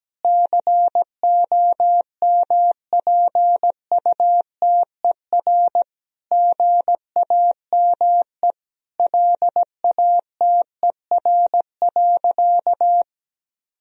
Morse code, heard as a keyed tone, about 17 wpm, 700 Hz. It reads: COMPUTER GAME LATER.
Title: Sentences from Top 1000 Words - Review Part 3 - 17wpm